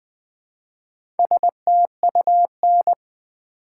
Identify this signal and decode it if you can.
Morse code, heard as a keyed tone, about 20 wpm, 700 Hz. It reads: STUN
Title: Build A Word - 1 Letter At A Time - 4-Letter Words - 20wpm